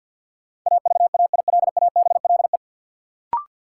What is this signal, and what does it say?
Morse code, heard as a keyed tone, about 52 wpm, 700 Hz. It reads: AVAILABLE
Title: Sentences from Top 600 Words - New Word Drill - 50wpm